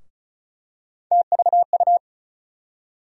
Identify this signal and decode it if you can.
Morse code, heard as a keyed tone, about 36 wpm, 700 Hz. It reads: TVU
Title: USA Call Sign Postfixes - 3 Letters 35wpm